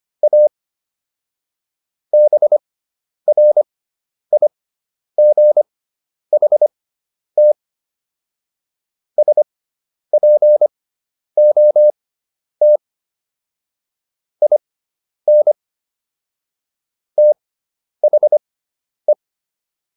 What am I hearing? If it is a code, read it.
Morse code, heard as a keyed tone, about 25 wpm, spaced as for 10 wpm, 600 Hz. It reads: A BRIGHT SPOT IN THE